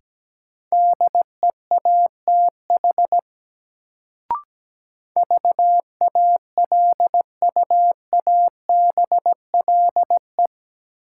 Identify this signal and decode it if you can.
Morse code, heard as a keyed tone, about 17 wpm, 700 Hz. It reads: DEATH VALUABLE
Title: Sentences from Top 1100 Words - New Word Drill - 17wpm